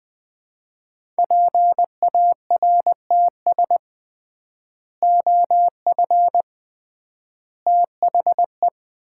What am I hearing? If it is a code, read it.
Morse code, heard as a keyed tone, about 20 wpm, 700 Hz. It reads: PARTS OF THE
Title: 3x Word-Spacing - Sets of 3 Words - 20wpm